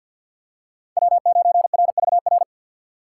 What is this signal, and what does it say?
Morse code, heard as a keyed tone, about 53 wpm, 700 Hz. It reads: W9RVR
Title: USA Call Signs - 5 Characters - 4x Character Spacing - 50wpm